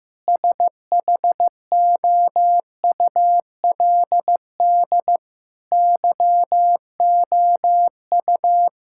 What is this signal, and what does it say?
Morse code, heard as a keyed tone, about 15 wpm, 700 Hz. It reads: SHOULD YOU
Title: Sentences from Top 1000 Words - Review Part 5 - 15wpm